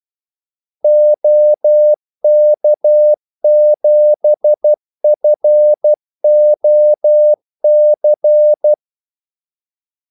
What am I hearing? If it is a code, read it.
Morse code, heard as a keyed tone, about 12 wpm, 600 Hz. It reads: OK7FOC
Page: Kaldesignaler 12 wpm